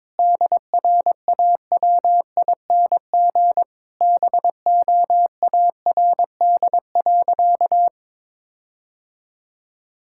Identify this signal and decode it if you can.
Morse code, heard as a keyed tone, about 22 wpm, 700 Hz. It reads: DRAWING BOARD.
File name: Sentences with 7-Word Phrases - 22wpm